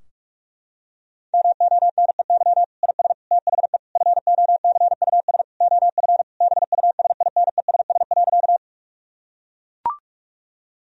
Morse code, heard as a keyed tone, about 42 wpm, 700 Hz. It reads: MONEY IS THE FOCUS OF BUSINESS.
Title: Sentences from Top 1000 Words - Review Part 4 - 45wpm